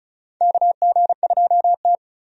Morse code, heard as a keyed tone, about 34 wpm, 700 Hz. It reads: KG2T
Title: USA Call Signs - 4 Characters - 35wpm